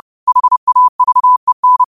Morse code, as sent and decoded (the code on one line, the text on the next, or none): .... .- ...- . -.
HAVEN